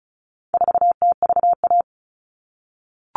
Morse code, heard as a keyed tone, about 34 wpm, 700 Hz. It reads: 4TVA
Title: All Cut Numbers - 4-Digit 35wpm